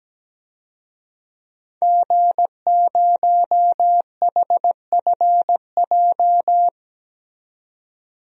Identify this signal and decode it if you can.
Morse code, heard as a keyed tone, about 17 wpm, 700 Hz. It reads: G0HFJ
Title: DXCC-100 Call Signs - 5 Characters - 17wpm